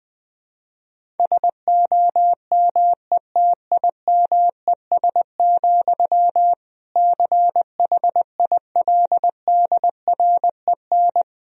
Morse code, heard as a keyed tone, about 21 wpm, 700 Hz. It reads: SOMETIMES, CHILDREN
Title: Sentences from Top 1000 Words - Review Part 3 - 20wpm